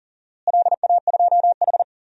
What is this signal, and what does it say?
Morse code, heard as a keyed tone, about 40 wpm, 700 Hz. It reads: LA2H
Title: DXCC-100 Call Signs - 4 Characters - Repeated 2x - 40wpm